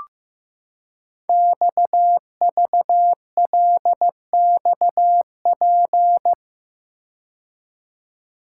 Morse code, heard as a keyed tone, about 15 wpm, 700 Hz. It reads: XVLXP
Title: Code Groups - 5 Letters - 15wpm